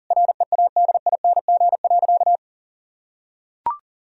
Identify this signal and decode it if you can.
Morse code, heard as a keyed tone, about 40 wpm, 700 Hz. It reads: READING.
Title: Sentences from Top 1000 Words - Review Part 3 - 40wpm